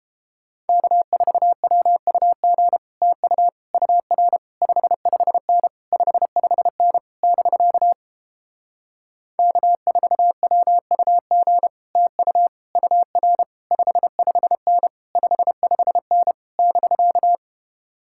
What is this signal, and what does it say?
Morse code, heard as a keyed tone, about 31 wpm, 700 Hz. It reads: K4WUG TU UR 55N 55N <BK> K4WUG TU UR 55N 55N <BK>